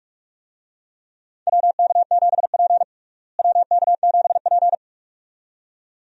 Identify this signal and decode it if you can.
Morse code, heard as a keyed tone, about 41 wpm, 700 Hz. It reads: WK7P WK7P
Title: USA Call Signs - 4 Characters - Repeated 2x - 45wpm